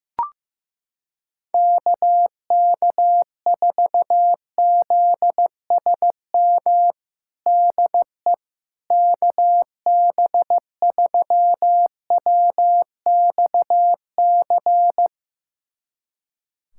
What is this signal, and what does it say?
Morse code, heard as a keyed tone, about 15 wpm, 700 Hz. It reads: KK4ZSM DE KB3WXC
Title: USA Callsign DE Callsign - 6 Characters - 15wpm